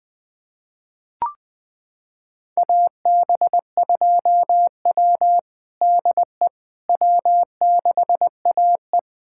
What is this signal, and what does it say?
Morse code, heard as a keyed tone, about 20 wpm, 700 Hz. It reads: AB2W DE W6AE
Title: USA Callsign DE Callsign - 4 Characters - 20wpm